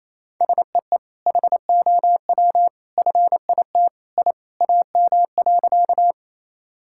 Morse code, heard as a keyed tone, about 28 wpm, 700 Hz. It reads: SEE HOW FIT I AM.
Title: Sentences from Top 1000 Words - Review Part 5 - 28wpm